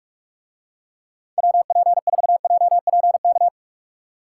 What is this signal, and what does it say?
Morse code, heard as a keyed tone, about 46 wpm, 700 Hz. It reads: WP4JPK